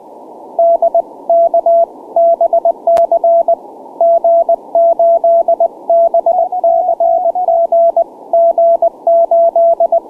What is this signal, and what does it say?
Morse code, heard as a keyed tone, about 20 wpm, 670 Hz. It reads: D K BC G8 BCG G8